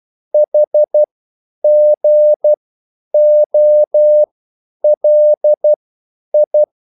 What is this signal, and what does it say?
Morse code, heard as a keyed tone, about 12 wpm, spaced as for 9 wpm, 600 Hz. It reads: HGOLI
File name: QSL / CW lesson